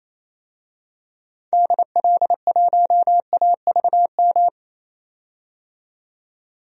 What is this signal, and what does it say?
Morse code, heard as a keyed tone, about 28 wpm, 700 Hz. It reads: DL1AVM
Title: DXCC-100 Call Signs - 6 Characters - 28wpm